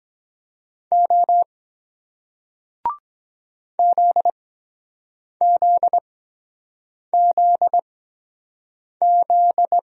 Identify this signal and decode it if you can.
Morse code, heard as a keyed tone, about 23 wpm, spaced as for 7 wpm, 700 Hz. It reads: O ZZZZ